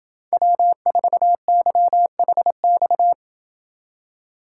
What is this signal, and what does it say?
Morse code, heard as a keyed tone, about 27 wpm, 700 Hz. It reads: W4YHX